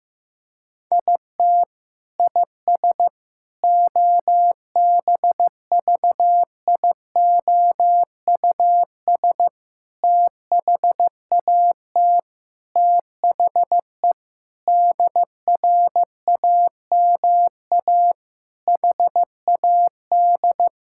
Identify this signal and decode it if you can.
Morse code, heard as a keyed tone, about 15 wpm, 700 Hz. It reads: IT IS OBVIOUS THAT THE DRAMA HAD
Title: Sentences from Top 1600 Words 15wpm